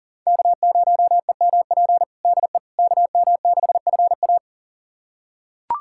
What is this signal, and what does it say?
Morse code, heard as a keyed tone, about 40 wpm, 700 Hz. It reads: K0EMP DE KM6FA
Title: USA Callsign DE Callsign - 5 Characters - 40wpm